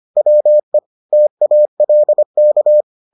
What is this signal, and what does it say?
Morse code, heard as a keyed tone, about 25 wpm, 600 Hz. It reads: WE TALK